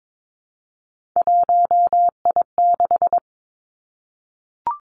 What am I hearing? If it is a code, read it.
Morse code, heard as a keyed tone, about 22 wpm, 700 Hz. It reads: 1I6
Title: Three Challenging Letter-Number Combinations 22wpm